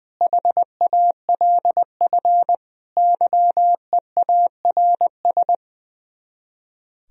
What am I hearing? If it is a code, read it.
Morse code, heard as a keyed tone, about 20 wpm, 700 Hz. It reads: HALF YEARS
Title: Speed-Racing - Sets of 5 Words 20wpm